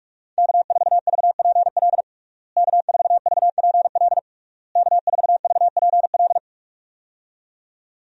Morse code, heard as a keyed tone, about 46 wpm, 700 Hz. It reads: K4VPL K4VPL K4VPL
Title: USA Call Signs - 5 Characters - Repeated 3x - 45wpm